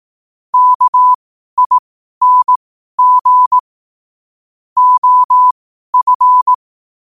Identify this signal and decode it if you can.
Morse code, heard as a keyed tone, about 18 wpm, spaced as for 12 wpm, 1000 Hz. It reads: KING OF
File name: qotdfiles / 12wpm